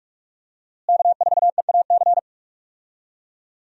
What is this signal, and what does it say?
Morse code, heard as a keyed tone, about 46 wpm, 700 Hz. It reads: K4EAC
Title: USA Call Signs - 5 Characters - Repeated 3x - 45wpm